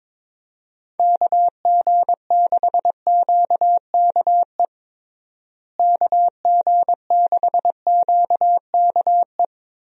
Morse code, heard as a keyed tone, about 22 wpm, 700 Hz. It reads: KG6QKE KG6QKE
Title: USA Call Signs - 6 Characters - Repeated 2x - 22wpm